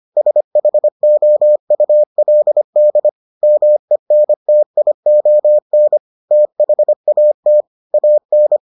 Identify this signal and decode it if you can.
Morse code, heard as a keyed tone, about 25 wpm, 600 Hz. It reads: SHOULD MENTION THAT AN